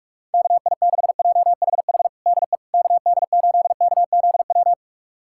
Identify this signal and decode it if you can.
Morse code, heard as a keyed tone, about 46 wpm, 700 Hz. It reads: KI6JHH DE KD8KZW
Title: USA Callsign DE Callsign 45wpm